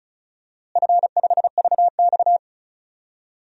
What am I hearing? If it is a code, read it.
Morse code, heard as a keyed tone, about 34 wpm, 700 Hz. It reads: F5VX
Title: DXCC-100 Call Signs - 4 Characters - Repeated 3x - 35wpm